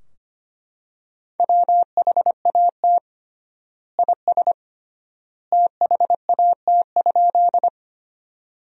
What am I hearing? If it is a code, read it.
Morse code, heard as a keyed tone, about 25 wpm, 700 Hz. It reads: WHAT IS THAT?
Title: 3x Word-Spacing - Sentences from Top 100 Words - 25wpm